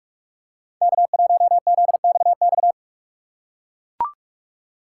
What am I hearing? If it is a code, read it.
Morse code, heard as a keyed tone, about 41 wpm, 700 Hz. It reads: K1ZXX
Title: USA Call Signs - 5 Characters - 2x Character Spacing - 45wpm